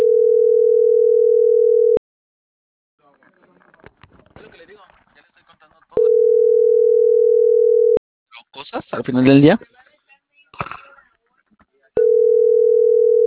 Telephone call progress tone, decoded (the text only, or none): ringback tone